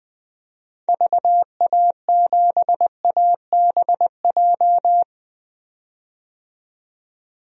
Morse code, heard as a keyed tone, about 20 wpm, 700 Hz. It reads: VA7ABJ
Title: DXCC-100 Call Signs - 6 Characters - 20wpm